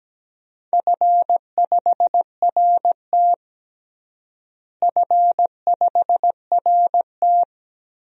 Morse code, heard as a keyed tone, about 17 wpm, 700 Hz. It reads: F5RT F5RT
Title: DXCC-100 Call Signs - 4 Characters - Repeated 3x - 17wpm